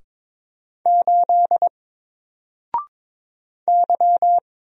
Morse code, heard as a keyed tone, about 22 wpm, 700 Hz. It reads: 8 Y
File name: Single-Letter-Number-22wpm